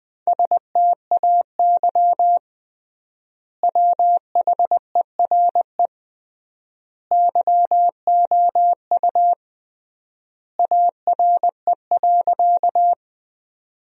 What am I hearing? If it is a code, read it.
Morse code, heard as a keyed tone, about 20 wpm, 700 Hz. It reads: STAY WHERE YOU ARE.
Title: 3x Word-Spacing - Sentences from Top 500 Words - 20wpm